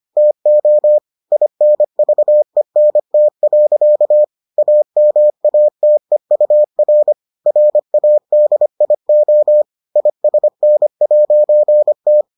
Morse code, heard as a keyed tone, about 25 wpm, 600 Hz. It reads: TO INVENT. AMATEUR RADIO ISN'T